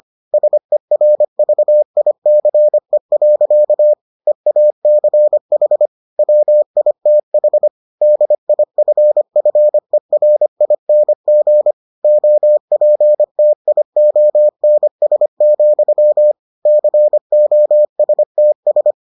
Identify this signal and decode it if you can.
Morse code, heard as a keyed tone, about 25 wpm, 600 Hz. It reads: SERVICE. EACH WITH DIFFERING OPTIONS, COSTS